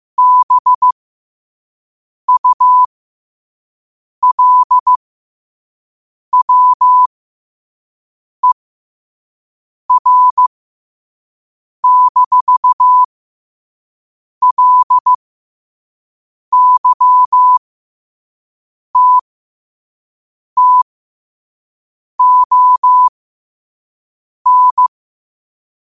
Morse code, heard as a keyed tone, about 15 wpm, spaced as for 5 wpm, 1000 Hz. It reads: BULWER-LYTTON
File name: qotdfiles / 5wpm